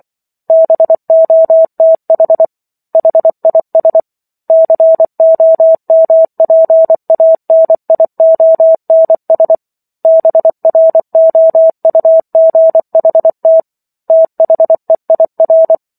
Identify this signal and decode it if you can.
Morse code, heard as a keyed tone, about 24 wpm, 650 Hz. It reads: BOTH HIS COMPANIONS BROUGHT THEIR